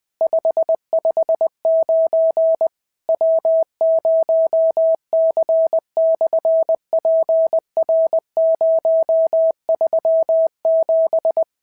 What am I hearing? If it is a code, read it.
Morse code, heard as a keyed tone, about 20 wpm, 650 Hz. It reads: 559 W0C/PR037
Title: SOTA QSO Practice 20wpm